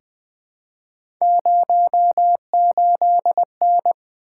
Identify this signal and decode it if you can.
Morse code, heard as a keyed tone, about 20 wpm, 700 Hz. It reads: 08N
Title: Three Letter-Number Combinations - 20wpm